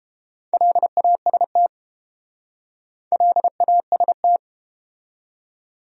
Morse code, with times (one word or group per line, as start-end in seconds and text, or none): 0.5-1.7 LAST
3.1-4.4 LAST